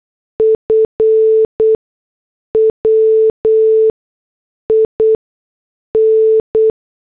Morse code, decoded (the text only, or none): F WIN